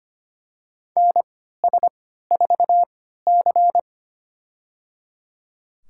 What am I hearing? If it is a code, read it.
Morse code, heard as a keyed tone, about 25 wpm, spaced as for 14 wpm, 700 Hz. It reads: NS4C
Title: USA Call Signs - 4 Characters - 3x Character Spacing - 25wpm